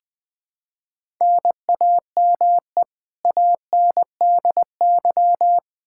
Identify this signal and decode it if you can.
Morse code, heard as a keyed tone, about 20 wpm, 700 Hz. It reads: NAME ANDY
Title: Practice QSO - Exchange 1 - Beginning - 20wpm